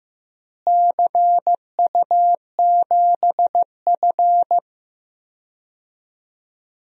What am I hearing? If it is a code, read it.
Morse code, heard as a keyed tone, about 15 wpm, 700 Hz. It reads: CU7F
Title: DXCC-100 Call Signs - 4 Characters - Repeated 2x - 15wpm